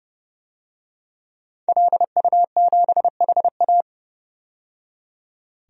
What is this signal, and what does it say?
Morse code, heard as a keyed tone, about 30 wpm, 700 Hz. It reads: LU7HA
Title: DXCC-100 Call Signs - 5 Characters - 30wpm